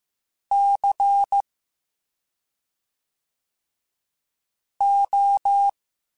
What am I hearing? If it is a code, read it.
Morse code, heard as a keyed tone, about 15 wpm, 800 Hz. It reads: C O